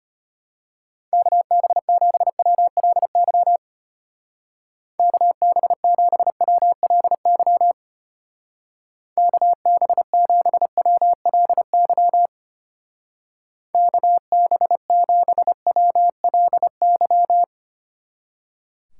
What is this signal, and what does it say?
Morse code, heard as a keyed tone, about 32 wpm, 700 Hz. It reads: KB7WLY KB7WLY KB7WLY KB7WLY